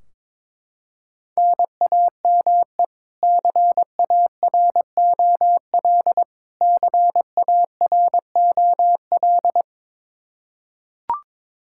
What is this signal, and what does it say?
Morse code, heard as a keyed tone, about 22 wpm, 700 Hz. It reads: NAME CAROL CAROL
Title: Practice QSO - Exchange 1 - Beginning - 22wpm